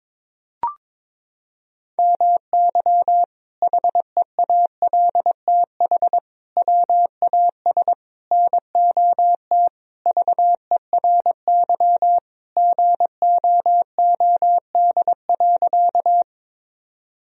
Morse code, heard as a keyed tone, about 22 wpm, 700 Hz. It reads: MY HEALTH WAS NOT VERY GOOD.
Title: Sentences from Top 1000 Words - Review Part 3 - 22wpm